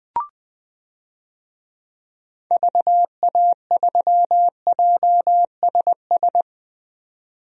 Morse code, heard as a keyed tone, about 20 wpm, 700 Hz. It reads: VA3JSS
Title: DXCC-100 Call Signs - 6 Characters - 20wpm